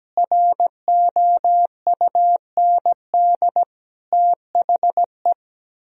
Morse code, heard as a keyed tone, about 17 wpm, 700 Hz. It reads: ROUND THE
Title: Sentences from Top 1000 Words - Review Part 4 - 17wpm